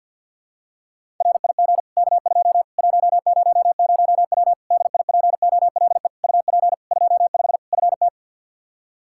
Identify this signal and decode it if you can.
Morse code, heard as a keyed tone, about 52 wpm, 700 Hz. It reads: RIG K2 100W DIPOLE UP 25 FT
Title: Practice QSO - Exchange 2 - Info - 50wpm